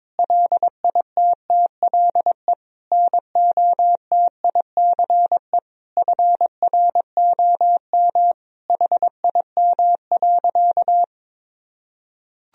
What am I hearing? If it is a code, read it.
Morse code, heard as a keyed tone, about 22 wpm, 700 Hz. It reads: LITTLE NOTICE FROM HIM.
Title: Sentences from Top 1000 Words - Review Part 2 - 22wpm